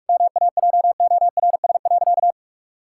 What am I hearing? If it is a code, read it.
Morse code, heard as a keyed tone, about 43 wpm, 700 Hz. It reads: MAJORS.
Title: Pangram Challenge v2 - 45wpm